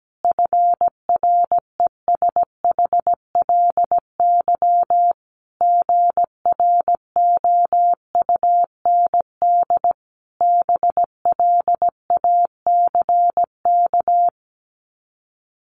Morse code, heard as a keyed tone, about 17 wpm, 700 Hz. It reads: FRESHLY GROUND BLACK
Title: Sets of 5 Words Encore 17wpm